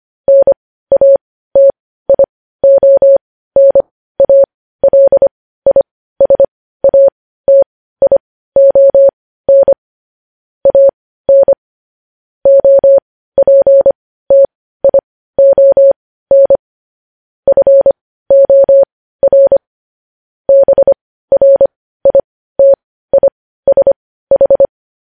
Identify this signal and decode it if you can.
Morse code, heard as a keyed tone, about 25 wpm, spaced as for 15 wpm, 550 Hz. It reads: NATIONALISATION AN OPTION FOR BRITISH
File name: podcast / News headlines-1x-25x15